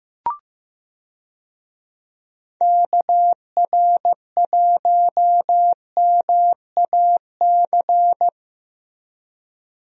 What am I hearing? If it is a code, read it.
Morse code, heard as a keyed tone, about 15 wpm, 700 Hz. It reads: KR1MAC
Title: USA Call Signs - 6 Characters - Repeated 3x - 15wpm